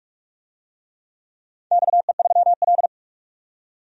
Morse code, heard as a keyed tone, about 47 wpm, 700 Hz. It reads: XE3L
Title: DXCC-100 Call Signs - 4 Characters - 45wpm